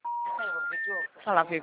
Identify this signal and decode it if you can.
Special information tone, with speech.